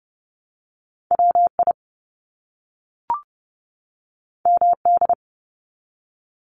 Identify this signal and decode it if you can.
Morse code, heard as a keyed tone, about 30 wpm, 700 Hz. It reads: WI MD
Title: US States 30wpm